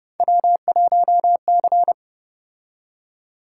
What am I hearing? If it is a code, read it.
Morse code, heard as a keyed tone, about 30 wpm, 700 Hz. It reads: W1C